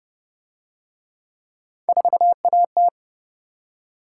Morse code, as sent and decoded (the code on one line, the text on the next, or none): ....- .- -
4AT